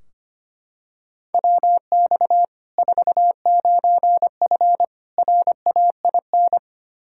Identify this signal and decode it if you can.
Morse code, heard as a keyed tone, about 25 wpm, 700 Hz. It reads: WX 49F RAIN
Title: Practice QSO - Exchange 2 - Info - 25wpm